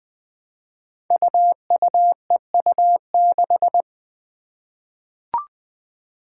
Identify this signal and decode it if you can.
Morse code, heard as a keyed tone, about 20 wpm, 700 Hz. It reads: UUEU6
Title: Five Challenging Letter-Number Combinations 20wpm